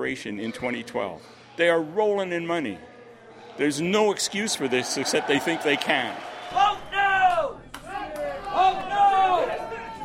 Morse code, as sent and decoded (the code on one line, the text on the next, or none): none